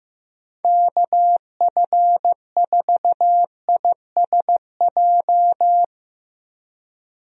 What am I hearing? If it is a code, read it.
Morse code, heard as a keyed tone, about 15 wpm, 700 Hz. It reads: KF4ISJ
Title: USA Call Signs - 6 Characters - Repeated 3x - 15wpm